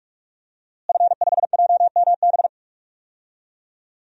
Morse code, heard as a keyed tone, about 41 wpm, 700 Hz. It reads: F5JMB
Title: DXCC-100 Call Signs - 5 Characters - Repeated 2x - 45wpm